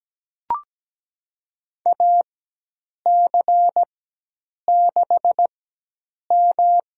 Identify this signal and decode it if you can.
Morse code, heard as a keyed tone, about 17 wpm, spaced as for 8 wpm, 700 Hz. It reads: AC6M
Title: USA Call Signs - 4x Character Spacing - 17wpm